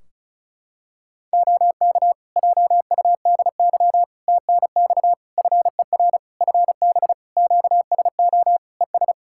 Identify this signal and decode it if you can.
Morse code, heard as a keyed tone, about 34 wpm, 700 Hz. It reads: OK JUDY TNX FER FB QSO ES
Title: Practice QSO - Exchange 4 - Ending - 35wpm